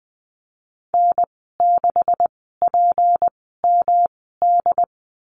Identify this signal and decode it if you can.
Morse code, heard as a keyed tone, about 20 wpm, 700 Hz. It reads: N 6 P M D
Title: 154 Lesson 25g - Letter p - Call Signs 20wpm